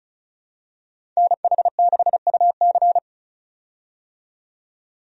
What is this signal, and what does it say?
Morse code, heard as a keyed tone, about 36 wpm, 700 Hz. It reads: NH6UC
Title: USA Call Signs - 5 Characters - 35wpm